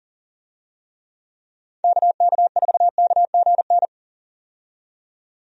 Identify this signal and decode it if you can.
Morse code, heard as a keyed tone, about 40 wpm, 700 Hz. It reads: KK4KGN
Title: USA Call Signs - 6 Characters - 40wpm